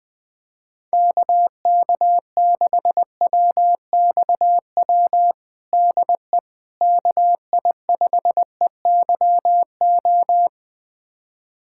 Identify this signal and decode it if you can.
Morse code, heard as a keyed tone, about 20 wpm, 700 Hz. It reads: KK6WXW DE KI5EYO
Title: USA Callsign DE Callsign - 6 Characters - 20wpm